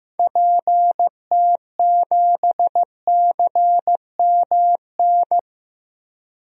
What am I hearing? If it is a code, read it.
Morse code, heard as a keyed tone, about 15 wpm, 700 Hz. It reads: PT7CMN
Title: DXCC-100 Call Signs - 6 Characters - Repeated 2x - 15wpm